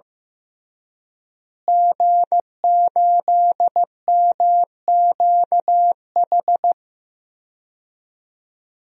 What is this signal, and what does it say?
Morse code, heard as a keyed tone, about 15 wpm, 700 Hz. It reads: G8MQH
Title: DXCC-100 Call Signs - 5 Characters - Repeated 3x - 15wpm